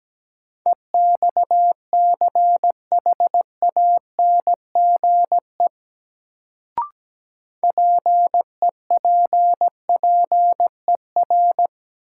Morse code, heard as a keyed tone, about 17 wpm, 700 Hz. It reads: EXCHANGE PEPPER
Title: Sets of 5 Words - New Word Drill - 17wpm